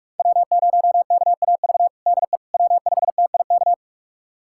Morse code, heard as a keyed tone, about 46 wpm, 700 Hz. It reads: W0KAV DE W5TIK
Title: USA Callsign DE Callsign - 5 Characters - 45wpm